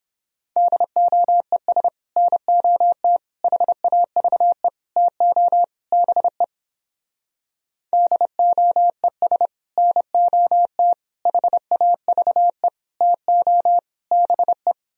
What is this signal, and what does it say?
Morse code, heard as a keyed tone, about 29 wpm, 700 Hz. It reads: DOES NOT HAVE TO BE DOES NOT HAVE TO BE